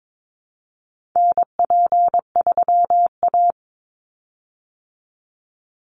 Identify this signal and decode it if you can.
Morse code, heard as a keyed tone, about 22 wpm, 700 Hz. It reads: NP3A
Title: Call Signs Repeated 3x - 22wpm